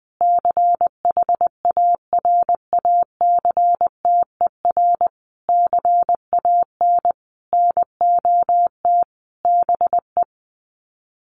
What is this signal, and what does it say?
Morse code, heard as a keyed tone, about 20 wpm, 700 Hz. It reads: CHARACTER CAN NOT BE
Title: Sets of 5 Words Encore2 20wpm